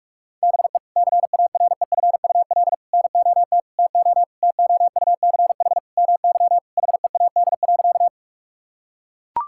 Morse code, heard as a keyed tone, about 41 wpm, 700 Hz. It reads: BE CAREFUL NOT TO TOUCH MY HEAD.
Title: Sentences from Top 1600 Words 45wpm